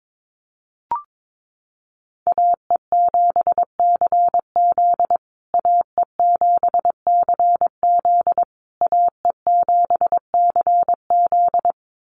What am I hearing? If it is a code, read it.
Morse code, heard as a keyed tone, about 22 wpm, 700 Hz. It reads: AE7CZ AE7CZ AE7CZ
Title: USA Call Signs Repeated 3x 22wpm